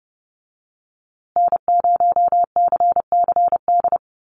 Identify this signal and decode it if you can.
Morse code, heard as a keyed tone, about 30 wpm, 700 Hz. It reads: N0CCD
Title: USA Call Signs Repeated 3x 30wpm